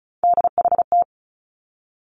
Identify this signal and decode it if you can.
Morse code, heard as a keyed tone, about 36 wpm, 700 Hz. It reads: DHT
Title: Three Challenging Letter-Number Combinations 35wpm